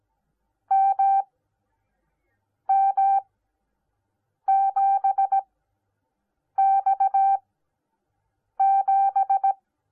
Morse code, heard as a keyed tone, about 17 wpm, spaced as for 6 wpm, 780 Hz. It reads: MM7X7